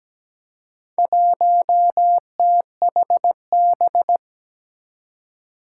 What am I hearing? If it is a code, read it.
Morse code, heard as a keyed tone, about 17 wpm, 700 Hz. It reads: 1THB
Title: Four Challenging Letter-Number Combinations 17wpm